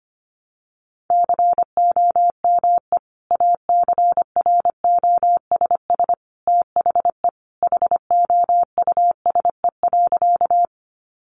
Morse code, heard as a keyed tone, about 25 wpm, 700 Hz. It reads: COME ACROSS THE HOUSE.
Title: Sentences from Top 700 Words 25wpm